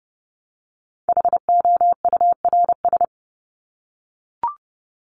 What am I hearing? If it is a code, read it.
Morse code, heard as a keyed tone, about 30 wpm, 700 Hz. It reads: HOURS
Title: ES:S Suffix - Top 100 Words with Base Words 30wpm